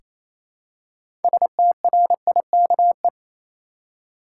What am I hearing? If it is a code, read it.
Morse code, heard as a keyed tone, about 27 wpm, 700 Hz. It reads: STRIKE
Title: Six Letter Words - 28wpm